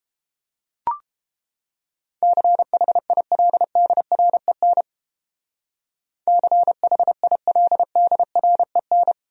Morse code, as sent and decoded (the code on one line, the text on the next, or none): -.-. .... .. .-.. -.. .-. -. / -.-. .... .. .-.. -.. .-. -.
CHILDREN CHILDREN